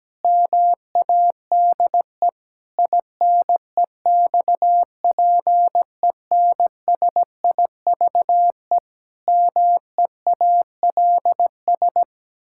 Morse code, heard as a keyed tone, about 17 wpm, 700 Hz. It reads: MADE INEXPENSIVE MEALS
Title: Pangram Challenge v2 - 17wpm